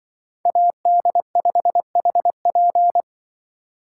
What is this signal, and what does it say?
Morse code, heard as a keyed tone, about 24 wpm, 700 Hz. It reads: AD5HP
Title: Speed Racing - USA 2x2 Call Signs 20wpm